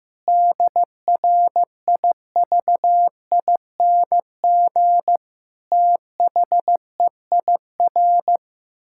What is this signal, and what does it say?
Morse code, heard as a keyed tone, about 15 wpm, 700 Hz. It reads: DRIVING THEIR
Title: Pangram Challenge v2 - 15wpm